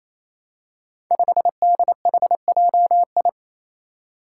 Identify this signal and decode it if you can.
Morse code, heard as a keyed tone, about 28 wpm, 700 Hz. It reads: 5DHJI
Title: Five Challenging Letter-Number Combinations 28wpm